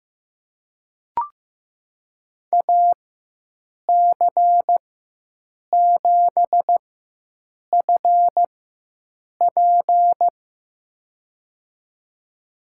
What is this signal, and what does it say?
Morse code, heard as a keyed tone, about 15 wpm, spaced as for 7 wpm, 700 Hz. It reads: AC7FP
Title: USA Call Signs - 5 Characters - 4x Character Spacing - 15wpm